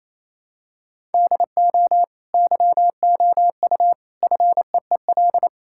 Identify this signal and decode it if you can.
Morse code, heard as a keyed tone, about 27 wpm, 700 Hz. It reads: DO YOU FEEL
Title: Sentences from Top 1000 Words - Review Part 5 - 28wpm